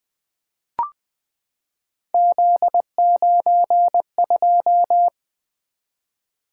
Morse code, heard as a keyed tone, about 20 wpm, 700 Hz. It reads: Z92
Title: Three Letter-Number Combinations - 20wpm